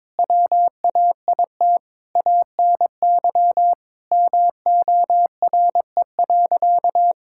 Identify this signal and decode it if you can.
Morse code, heard as a keyed tone, about 22 wpm, 700 Hz. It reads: WAIT ANY MORE.
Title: Sentences from Top 1000 Words - Review Part 5 - 22wpm